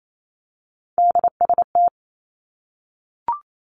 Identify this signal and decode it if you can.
Morse code, heard as a keyed tone, about 28 wpm, 700 Hz. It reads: DST
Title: Three Challenging Letter-Number Combinations 28wpm